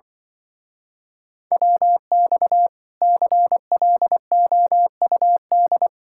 Morse code, heard as a keyed tone, about 24 wpm, 700 Hz. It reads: WX CLOUD
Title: Speed-Racing Common QSO Phrases - 20wpm